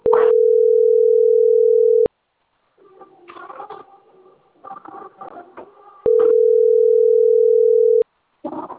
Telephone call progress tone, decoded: ringback tone